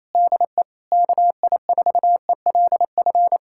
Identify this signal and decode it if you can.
Morse code, heard as a keyed tone, about 26 wpm, 700 Hz. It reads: DE KI4ELF
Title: Speed-Racing Common QSO Phrases - 28wpm